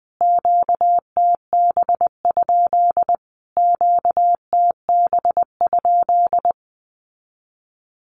Morse code, heard as a keyed tone, about 20 wpm, 700 Hz. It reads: QTB? QTB?
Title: Q-Codes Repeated 3x 20wpm